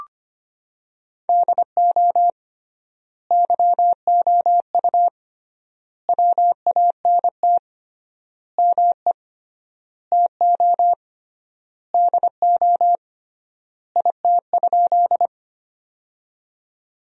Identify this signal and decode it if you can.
Morse code, heard as a keyed tone, about 25 wpm, 700 Hz. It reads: DO YOU WANT ME TO DO IT?
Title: 3x Word-Spacing - Sentences from Top 200 Words - 25wpm